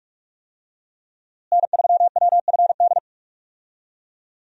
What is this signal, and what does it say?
Morse code, heard as a keyed tone, about 42 wpm, 700 Hz. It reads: N3WFD
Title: USA Call Signs - 5 Characters - Repeated 2x - 45wpm